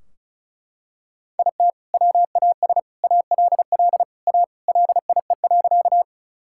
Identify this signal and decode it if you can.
Morse code, heard as a keyed tone, about 36 wpm, 700 Hz. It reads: IT WAS ALL A LIE.
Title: Sentences from Top 1600 Words 35wpm